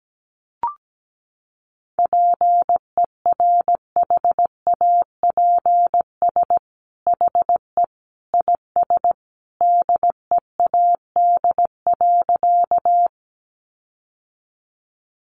Morse code, heard as a keyed tone, about 17 wpm, 700 Hz. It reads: PERHAPS HE IS DEAD.
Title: Sentences from Top 1500 Words 17wpm